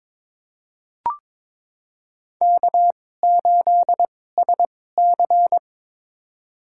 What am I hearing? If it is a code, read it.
Morse code, heard as a keyed tone, about 22 wpm, spaced as for 16 wpm, 700 Hz. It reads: K8SC
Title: USA Call Signs - 4 Characters - 2x Character Spacing - 22wpm